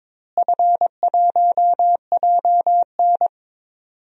Morse code, heard as a keyed tone, about 22 wpm, 700 Hz. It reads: F1JN
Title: DXCC-100 Call Signs - 4 Characters - Repeated 3x - 22wpm